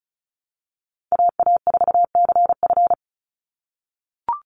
Morse code, heard as a keyed tone, about 34 wpm, 700 Hz. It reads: AA4CF
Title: Call Signs - 35wpm